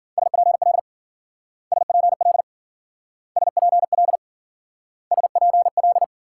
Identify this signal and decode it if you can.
Morse code, heard as a keyed tone, about 50 wpm, 700 Hz. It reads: SPL SPL SPL SPL
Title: Speed-Racing Multi-letter Phonemes - 40wpm